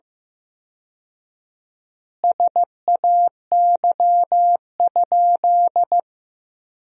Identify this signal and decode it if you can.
Morse code, heard as a keyed tone, about 15 wpm, 700 Hz. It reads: SAY?
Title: 4x Word-Spacing - Sentences from Top 400 Words - 15wpm